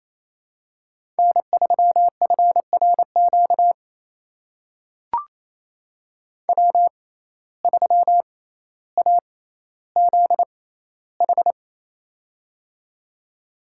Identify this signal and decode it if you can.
Morse code, heard as a keyed tone, about 28 wpm, spaced as for 14 wpm, 700 Hz. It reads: N3FRQ W3AZH
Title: USA Call Signs - 5 Characters - 6x Character Spacing - 28wpm